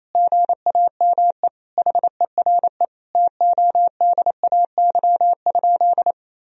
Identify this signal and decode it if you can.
Morse code, heard as a keyed tone, about 28 wpm, 700 Hz. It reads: GAME HERE TODAY?
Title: Sentences from Top 1000 Words - Review Part 3 - 28wpm